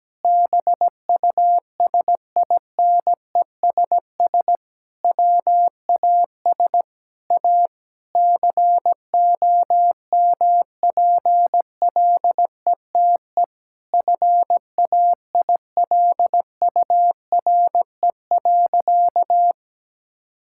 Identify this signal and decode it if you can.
Morse code, heard as a keyed tone, about 17 wpm, 700 Hz. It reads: BUSINESS WAS A COMPLETE FAILURE.
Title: Sentences from Top 1000 Words - Review Part 6 - 17wpm